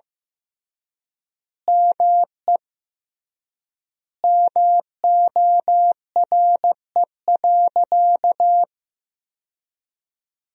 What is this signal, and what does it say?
Morse code, heard as a keyed tone, about 15 wpm, 700 Hz. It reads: ME MORE.
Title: 3x Word-Spacing - Sentences from Top 300 Words - 15wpm